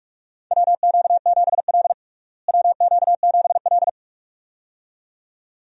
Morse code, heard as a keyed tone, about 46 wpm, 700 Hz. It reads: WQ7L WQ7L
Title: USA Call Signs - 4 Characters - Repeated 2x - 45wpm